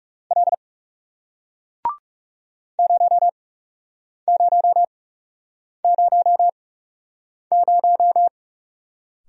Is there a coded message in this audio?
Yes: Morse code from 0.3 to 8.3 s, heard as a keyed tone, about 41 wpm, spaced as for 8 wpm, 700 Hz. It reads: L 0000